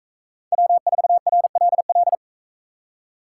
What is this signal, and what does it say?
Morse code, heard as a keyed tone, about 42 wpm, 700 Hz. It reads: W4RLL